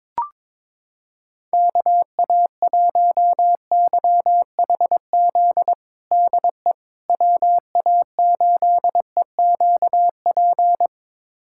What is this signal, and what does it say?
Morse code, heard as a keyed tone, about 22 wpm, 700 Hz. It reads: KA1YHZ DE WA8EQP
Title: USA Callsign DE Callsign - 6 Characters - 22wpm